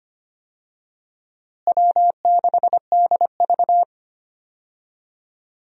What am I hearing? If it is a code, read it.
Morse code, heard as a keyed tone, about 25 wpm, 700 Hz. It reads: W6DV
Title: USA Call Signs - 4 Characters - Repeated 2x - 25wpm